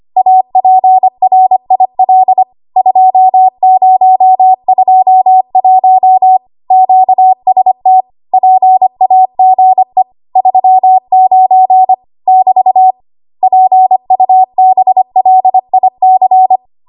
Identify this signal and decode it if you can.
Morse code, heard as a keyed tone, about 25 wpm, 750 Hz. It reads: APRIL 2021 QST PAGE 39 <BT> PUBLIC
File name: Archive / 25 WPM / 25WPM